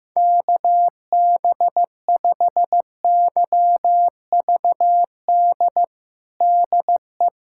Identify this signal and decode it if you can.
Morse code, heard as a keyed tone, about 16 wpm, 700 Hz. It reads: KB5YVD DE
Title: USA Callsign DE Callsign - 6 Characters - 15wpm